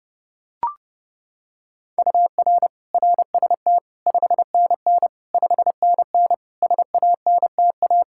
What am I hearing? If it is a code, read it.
Morse code, heard as a keyed tone, about 30 wpm, 700 Hz. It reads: UR RST 5NN 5NN SANTA
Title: Practice QSO - Exchange 1 - Beginning - 30wpm